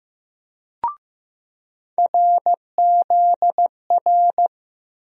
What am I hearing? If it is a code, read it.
Morse code, heard as a keyed tone, about 15 wpm, 700 Hz. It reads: RZR
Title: USA Call Sign Postfixes - 3 Letters 15wpm